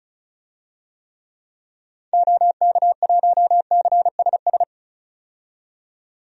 Morse code, heard as a keyed tone, about 34 wpm, 700 Hz. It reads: OK1CSS
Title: DXCC-100 Call Signs - 6 Characters - 35wpm